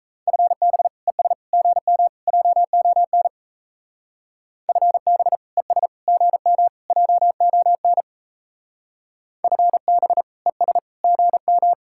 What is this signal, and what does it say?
Morse code, heard as a keyed tone, about 35 wpm, 700 Hz. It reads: FB ES GM JON FB ES GM JON FB ES GM